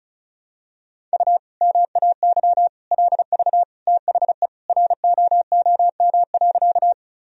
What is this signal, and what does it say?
Morse code, heard as a keyed tone, about 34 wpm, 700 Hz. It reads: U MAY LV THE ROOM.
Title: Sentences from Top 400 Words and Abbreviations 35wpm